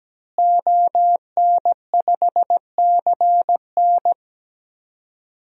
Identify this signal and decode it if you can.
Morse code, heard as a keyed tone, about 17 wpm, 700 Hz. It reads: ON5CN
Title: DXCC-100 Call Signs - 5 Characters - Repeated 3x - 17wpm